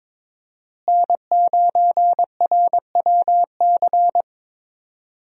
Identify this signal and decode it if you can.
Morse code, heard as a keyed tone, about 22 wpm, 700 Hz. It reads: N9RWC
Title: USA Call Signs - 5 Characters - Repeated 3x - 22wpm